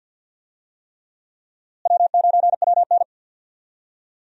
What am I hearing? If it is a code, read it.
Morse code, heard as a keyed tone, about 51 wpm, 700 Hz. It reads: W9WN
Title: USA Call Signs - 4 Characters - Repeated 3x - 50wpm